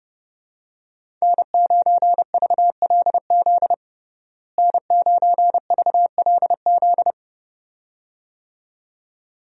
Morse code, heard as a keyed tone, about 30 wpm, 700 Hz. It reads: N9VLZ N9VLZ
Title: USA Call Signs - 5 Characters - Repeated 2x - 30wpm